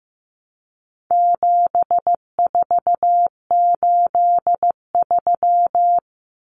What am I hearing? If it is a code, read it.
Morse code, heard as a keyed tone, about 15 wpm, 700 Hz. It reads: 7483
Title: Common Cut Numbers - 4-Digit 15wpm